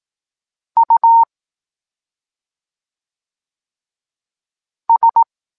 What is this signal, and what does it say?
Morse code, heard as a keyed tone, about 18 wpm, 920 Hz. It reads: U S
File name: Tekst-26 1